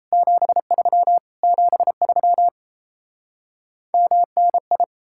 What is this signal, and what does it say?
Morse code, heard as a keyed tone, about 37 wpm, 700 Hz. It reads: 73 73 MNI